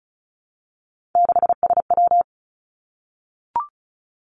Morse code, heard as a keyed tone, about 36 wpm, 700 Hz. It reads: 6SW
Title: Three Challenging Letter-Number Combinations 35wpm